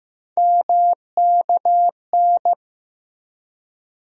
Morse code, heard as a keyed tone, about 15 wpm, 700 Hz. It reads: MKN
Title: USA Call Sign Postfixes - 3 Letters 15wpm